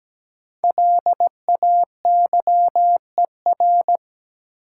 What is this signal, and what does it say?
Morse code, heard as a keyed tone, about 17 wpm, 700 Hz. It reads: LAYER
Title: Sentences from Top 1600 Words - New Word Drill - 17wpm